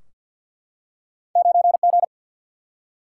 Morse code, heard as a keyed tone, about 53 wpm, 700 Hz. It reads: 9G
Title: Single-Letter Single-Number 50wpm